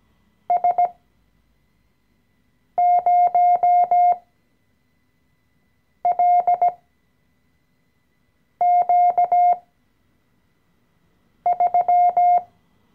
Morse code, heard as a keyed tone, about 17 wpm, spaced as for 4 wpm, 710 Hz. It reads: S0LQ3
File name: CW-Online-Kurs / Anfänger / Lektion 014